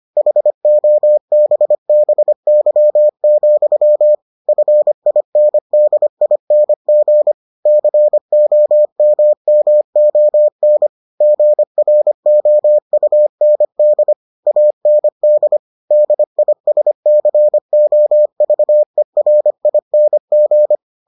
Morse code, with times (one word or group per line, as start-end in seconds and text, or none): 0.2-4.2 HOBBY,
4.5-7.3 FINDING
7.7-10.9 COMMON
11.2-14.1 GROUND
14.5-15.6 AND
15.9-20.8 DISCOVERING